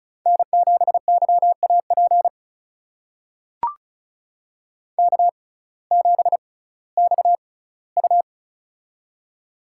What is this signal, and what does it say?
Morse code, heard as a keyed tone, about 36 wpm, spaced as for 19 wpm, 700 Hz. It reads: N7YAP K7XU
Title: USA Call Signs - 6x Character Spacing - 35wpm